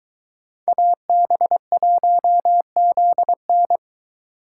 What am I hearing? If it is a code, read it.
Morse code, heard as a keyed tone, about 22 wpm, 700 Hz. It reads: AB1ZN